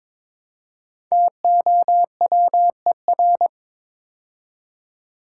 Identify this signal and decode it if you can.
Morse code, heard as a keyed tone, about 23 wpm, 700 Hz. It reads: TOWER
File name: practice / 22wpm / 5 Letter Words - First 3 Letters Given - 22wpm